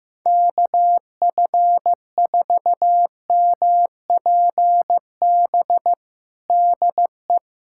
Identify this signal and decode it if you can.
Morse code, heard as a keyed tone, about 15 wpm, 700 Hz. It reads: KF4MPB DE
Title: USA Callsign DE Callsign 15wpm